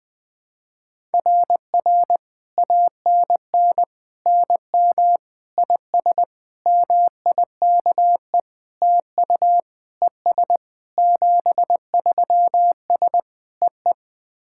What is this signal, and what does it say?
Morse code, heard as a keyed tone, about 20 wpm, 700 Hz. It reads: RR ANN NM IS MIKE TU ES 73S EE
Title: SOTA QSO Practice 20wpm